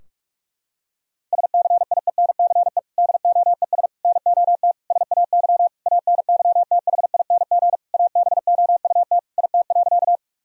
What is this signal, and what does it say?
Morse code, heard as a keyed tone, about 46 wpm, 700 Hz. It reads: SCIENCE DOES NOT SAY ANYTHING ABOUT IT.
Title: Sentences from Top 1000 Words - Review Part 2 - 45wpm